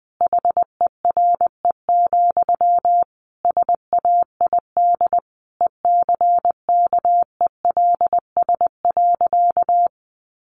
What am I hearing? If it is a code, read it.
Morse code, heard as a keyed tone, about 20 wpm, 700 Hz. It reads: HERE, SAID ECKELS.
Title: A Sound Of Thunder 20wpm